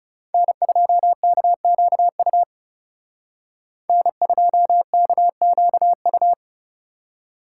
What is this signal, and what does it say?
Morse code, heard as a keyed tone, about 34 wpm, 700 Hz. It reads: N2KQU N2KQU
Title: Speed Racing - USA 1x3 Call Signs 30wpm